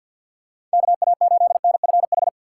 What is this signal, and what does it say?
Morse code, heard as a keyed tone, about 47 wpm, 700 Hz. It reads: KA8NFH
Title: USA Call Signs - 6 Characters - 3x Character Spacing - 50wpm